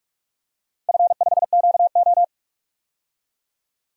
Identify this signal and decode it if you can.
Morse code, heard as a keyed tone, about 42 wpm, 700 Hz. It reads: F5QO
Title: DXCC-100 Call Signs - 4 Characters - 45wpm